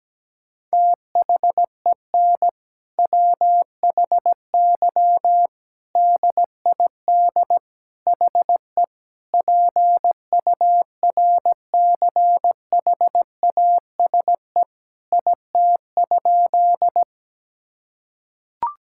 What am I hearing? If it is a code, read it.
Morse code, heard as a keyed tone, about 17 wpm, 700 Hz. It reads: THEN WHY DID HE PURCHASE IT?
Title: Sentences from Top 1000 Words - Review Part 5 - 17wpm